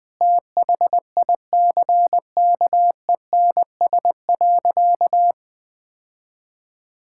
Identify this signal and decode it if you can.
Morse code, heard as a keyed tone, about 20 wpm, 700 Hz. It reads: THICKENS.
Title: Sentences with 5-Word Phrases - Spoken Beginning - 20wpm